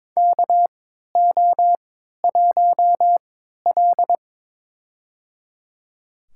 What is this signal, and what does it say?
Morse code, heard as a keyed tone, about 22 wpm, spaced as for 13 wpm, 700 Hz. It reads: KO1L
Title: USA Call Signs - 4 Characters - 3x Character Spacing - 22wpm